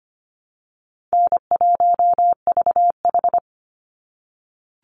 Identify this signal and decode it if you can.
Morse code, heard as a keyed tone, about 25 wpm, 700 Hz. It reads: N1VH
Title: Easy Call Signs - 25wpm